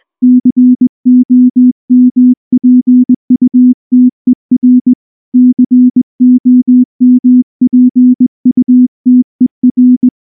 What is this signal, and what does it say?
Morse code, heard as a keyed tone, about 19 wpm, 250 Hz. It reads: COMPUTER COMPUTER
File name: morse / morse-training-20WPM-1